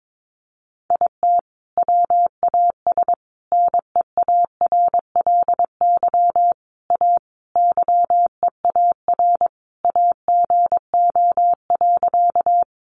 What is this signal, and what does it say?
Morse code, heard as a keyed tone, about 22 wpm, 700 Hz. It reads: IT WAS NEARLY A YEAR AGO.
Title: Sentences from Top 1300 Words 22wpm